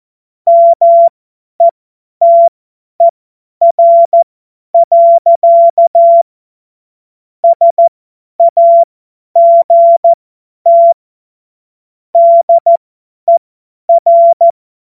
Morse code, heard as a keyed tone, about 14 wpm, spaced as for 10 wpm, 680 Hz. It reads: METER. SAGT DER
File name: Morsekurs / Witze / Witz 28. Dezember 2021, 50 F70